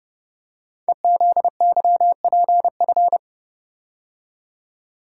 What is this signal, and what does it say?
Morse code, heard as a keyed tone, about 30 wpm, 690 Hz. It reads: EZYPF
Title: Code Groups - 5 Letters - 30wpm